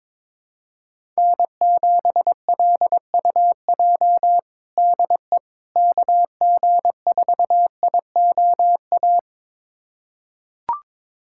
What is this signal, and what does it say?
Morse code, heard as a keyed tone, about 22 wpm, 700 Hz. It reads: N7LUJ DE KG4IOA
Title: USA Callsign DE Callsign 22wpm